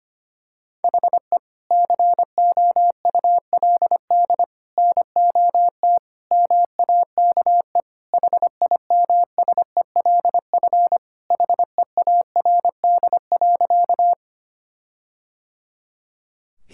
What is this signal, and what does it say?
Morse code, heard as a keyed tone, about 25 wpm, 700 Hz. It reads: HE COULD NOT MAKE HIMSELF HEARD.
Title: Sentences from Top 1600 Words 25wpm